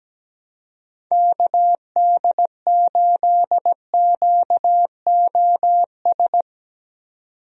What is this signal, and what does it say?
Morse code, heard as a keyed tone, about 17 wpm, 700 Hz. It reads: KD8QOS
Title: USA Call Signs - 6 Characters - 17wpm